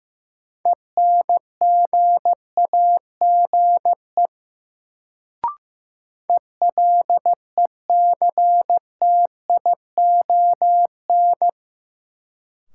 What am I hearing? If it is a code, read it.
Morse code, heard as a keyed tone, about 15 wpm, 700 Hz. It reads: ENGAGE ELECTION
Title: Sentences from Top 1500 Words - New Word Drill Alt - 15wpm